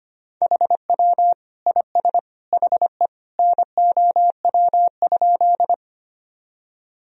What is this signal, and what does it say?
Morse code, heard as a keyed tone, about 25 wpm, 700 Hz. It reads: HW IS HE NOW?
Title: Sentences from Top 100 Words and Abbreviations 25wpm